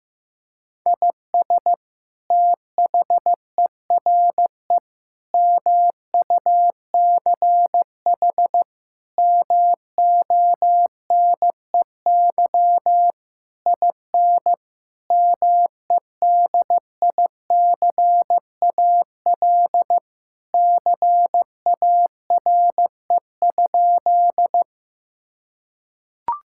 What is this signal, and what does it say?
Morse code, heard as a keyed tone, about 15 wpm, 700 Hz. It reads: IS THERE MUCH MONEY IN MEDICAL CARE?
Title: Sentences from Top 1000 Words - Review Part 5 - 15wpm